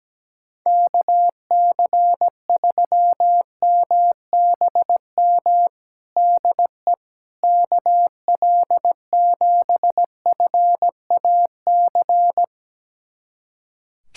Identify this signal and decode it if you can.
Morse code, heard as a keyed tone, about 17 wpm, 700 Hz. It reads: KC3MBM DE KL7FAC
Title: USA Callsign DE Callsign 17wpm